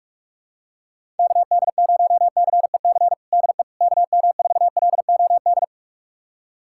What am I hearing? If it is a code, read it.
Morse code, heard as a keyed tone, about 46 wpm, 700 Hz. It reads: KD0CEC DE KM4LOD
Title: USA Callsign DE Callsign - 6 Characters - 45wpm